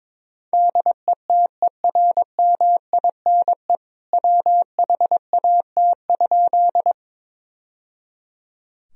Morse code, heard as a keyed tone, about 21 wpm, 700 Hz. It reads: DETERMINE WHAT?
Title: Sentences from Top 1000 Words - Review Part 4 - 22wpm